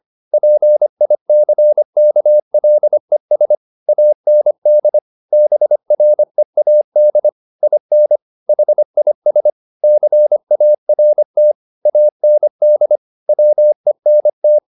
Morse code, heard as a keyed tone, about 25 wpm, 600 Hz. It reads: PICKLES AND BREAD IN HIS CART AND WENT